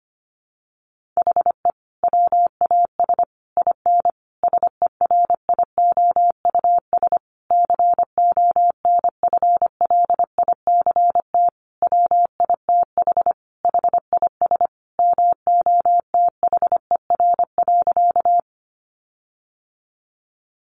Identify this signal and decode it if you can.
Morse code, heard as a keyed tone, about 25 wpm, 700 Hz. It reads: HE WAS IN SERIOUS CONFLICT WITH HIS MOTHER.
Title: Sentences from Top 1300 Words 25wpm